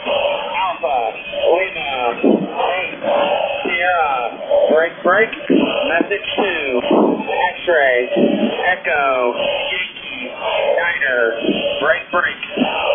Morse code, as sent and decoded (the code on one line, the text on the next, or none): none